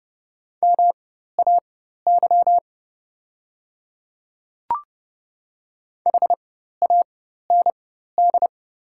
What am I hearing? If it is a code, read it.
Morse code, heard as a keyed tone, about 30 wpm, spaced as for 14 wpm, 700 Hz. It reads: MAY HAND
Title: 4x Character Spacing - Top 200 Words - 30wpm